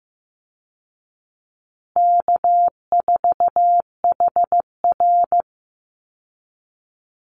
Morse code, heard as a keyed tone, about 15 wpm, 700 Hz. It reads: K4HR
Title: Easy Call Signs Repeated 2x - 15wpm